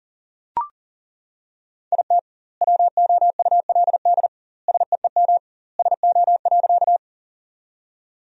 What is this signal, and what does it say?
Morse code, heard as a keyed tone, about 40 wpm, 700 Hz. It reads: IT WOULD SEEM SO.
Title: 2x Word-Spacing - Sentences from Top 300 Words - 40wpm